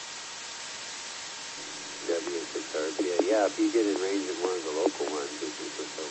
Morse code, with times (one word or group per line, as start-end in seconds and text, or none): none